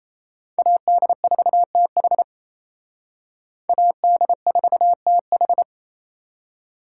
Morse code, heard as a keyed tone, about 37 wpm, 700 Hz. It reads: AD4TH AD4TH